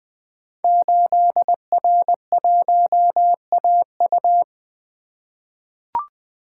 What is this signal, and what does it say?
Morse code, heard as a keyed tone, about 20 wpm, 700 Hz. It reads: 8R1AU
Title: Code Groups - 5 Letter:Numbers - 20wpm